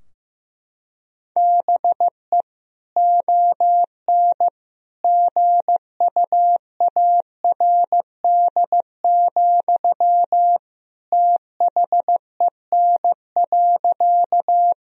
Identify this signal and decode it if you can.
Morse code, heard as a keyed tone, about 15 wpm, 700 Hz. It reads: BE ON GUARD, THEN.
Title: Sentences from Top 1000 Words - Review Part 4 - 15wpm